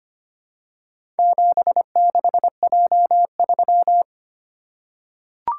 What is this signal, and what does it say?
Morse code, heard as a keyed tone, about 25 wpm, 700 Hz. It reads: 76J3
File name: practice / 25wpm / Four Letter-Number Combinations - 25wpm